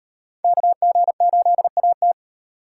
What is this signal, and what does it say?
Morse code, heard as a keyed tone, about 38 wpm, 700 Hz. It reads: KG8AT